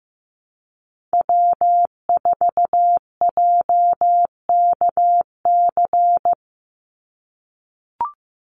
Morse code, heard as a keyed tone, about 15 wpm, 700 Hz. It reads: W4JKC
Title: NA SOTA Chaser Callsigns 15wpm